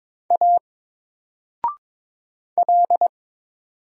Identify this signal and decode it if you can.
Morse code, heard as a keyed tone, about 22 wpm, 700 Hz. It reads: A L
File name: Single-Letter-Number-22wpm